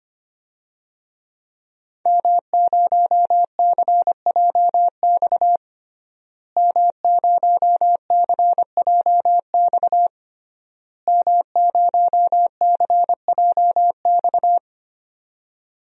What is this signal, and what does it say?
Morse code, heard as a keyed tone, about 25 wpm, 700 Hz. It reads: M0CJX M0CJX M0CJX
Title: DXCC-100 Call Signs - 5 Characters - Repeated 3x - 25wpm